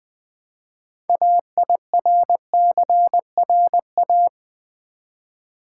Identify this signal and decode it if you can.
Morse code, heard as a keyed tone, about 20 wpm, 700 Hz. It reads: AIRCRA
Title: Build A Word - 2 Letters At A Time - 8-Letter Words - 20wpm